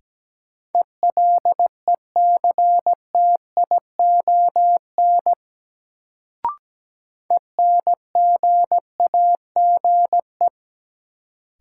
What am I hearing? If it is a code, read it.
Morse code, heard as a keyed tone, about 17 wpm, 700 Hz. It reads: ELECTION ENGAGE
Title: Sentences from Top 1500 Words - New Word Drill - 17wpm